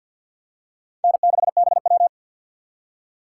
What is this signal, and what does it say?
Morse code, heard as a keyed tone, about 50 wpm, 700 Hz. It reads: N6BW
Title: USA Call Signs - 6x Character Spacing - 50wpm